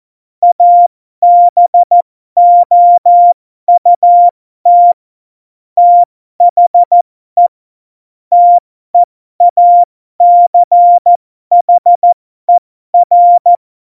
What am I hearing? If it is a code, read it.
Morse code, heard as a keyed tone, about 14 wpm, spaced as for 12 wpm, 700 Hz. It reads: ABOUT THE TEACHER
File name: Morsekurs / Witze